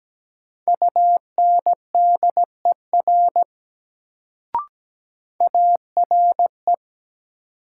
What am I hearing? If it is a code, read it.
Morse code, heard as a keyed tone, about 17 wpm, 700 Hz. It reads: UNDER ARE
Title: Sets of 2 Words - New Word Drill - 17wpm